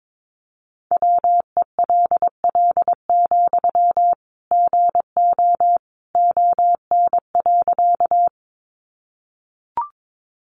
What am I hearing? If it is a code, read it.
Morse code, heard as a keyed tone, about 22 wpm, 700 Hz. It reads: WELL, GO ON.
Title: Sentences from Top 200 Words 22wpm